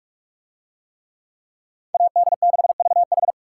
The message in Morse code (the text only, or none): AD6VH